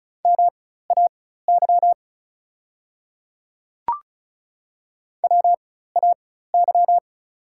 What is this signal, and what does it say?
Morse code, heard as a keyed tone, about 36 wpm, spaced as for 16 wpm, 690 Hz. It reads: MAY WAY
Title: 4x Character Spacing - Top 200 Words - 35wpm